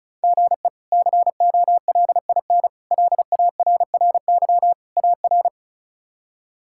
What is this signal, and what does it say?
Morse code, heard as a keyed tone, about 34 wpm, 700 Hz. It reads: GE COLIN LARRY AR
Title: K1USN SST QSO Practice - 35wpm